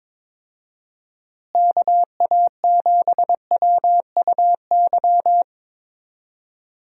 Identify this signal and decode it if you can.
Morse code, heard as a keyed tone, about 21 wpm, 700 Hz. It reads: KA7WUY
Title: USA Call Signs - 6 Characters - Repeated 2x - 22wpm